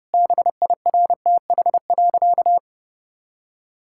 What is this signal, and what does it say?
Morse code, heard as a keyed tone, about 30 wpm, 700 Hz. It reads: BIRTH.
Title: Sentences from Top 1000 Words - Review Part 5 - 30wpm